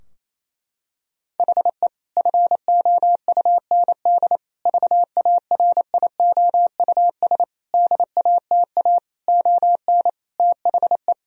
Morse code, heard as a keyed tone, about 28 wpm, 700 Hz. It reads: HE FOUND VARIOUS DATA ON THE
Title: Sentences from Top 1000 Words - Review Part 4 - 28wpm